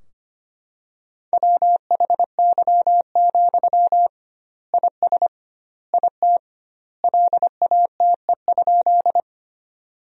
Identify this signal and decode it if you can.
Morse code, heard as a keyed tone, about 25 wpm, 700 Hz. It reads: WHY, IS IT LATE?
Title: 2x Word-Spacing - Sentences from Top 300 Words - 25wpm